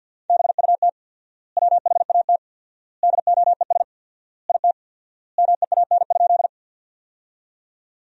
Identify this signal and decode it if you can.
Morse code, heard as a keyed tone, about 50 wpm, 700 Hz. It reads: BUT WHAT DOES IT MEAN?
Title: 4x Word-Spacing - Sentences from Top 200 Words - 50wpm